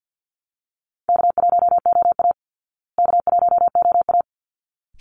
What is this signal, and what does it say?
Morse code, heard as a keyed tone, about 49 wpm, 700 Hz. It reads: K1OA K1OA
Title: Call Signs Repeated 3x - 50wpm